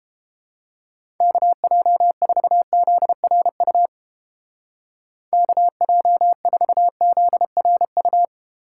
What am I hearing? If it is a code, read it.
Morse code, heard as a keyed tone, about 29 wpm, 700 Hz. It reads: KJ4ZRU KJ4ZRU